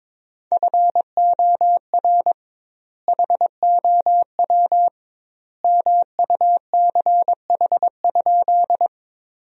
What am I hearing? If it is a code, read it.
Morse code, heard as a keyed tone, about 22 wpm, 700 Hz. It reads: FOR HOW MUCH?
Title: 2x Word-Spacing - Sentences from Top 200 Words - 22wpm